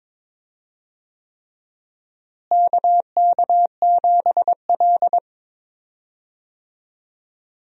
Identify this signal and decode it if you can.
Morse code, heard as a keyed tone, about 21 wpm, 700 Hz. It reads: KK7L
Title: USA Call Signs - 4 Characters - Repeated 3x - 22wpm